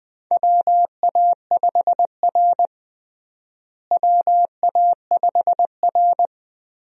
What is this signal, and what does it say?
Morse code, heard as a keyed tone, about 20 wpm, 700 Hz. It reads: WA5R WA5R
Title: USA Call Signs - 4 Characters - Repeated 2x - 20wpm